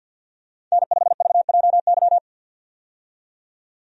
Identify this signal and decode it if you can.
Morse code, heard as a keyed tone, about 51 wpm, 700 Hz. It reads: N5VJY
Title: USA Call Signs - 5 Characters - 50wpm